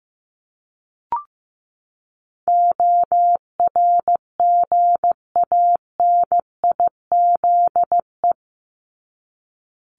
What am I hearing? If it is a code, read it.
Morse code, heard as a keyed tone, about 15 wpm, 700 Hz. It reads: ORGANIZE
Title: Top 1751-2000 Words 15wpm